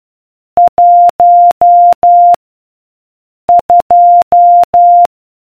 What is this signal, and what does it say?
Morse code, heard as a keyed tone, about 12 wpm, 700 Hz. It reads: 1 2